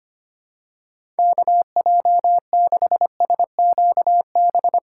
Morse code, heard as a keyed tone, about 25 wpm, 700 Hz. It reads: KJ6SQB
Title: USA Call Signs - 6 Characters - 2x Character Spacing - 25wpm